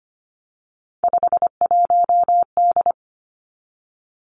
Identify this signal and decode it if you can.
Morse code, heard as a keyed tone, about 25 wpm, 700 Hz. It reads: 51D
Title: Three Challenging Letter-Number Combinations 25wpm